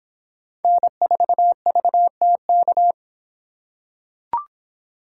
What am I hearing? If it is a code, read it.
Morse code, heard as a keyed tone, about 26 wpm, 700 Hz. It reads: N4VTK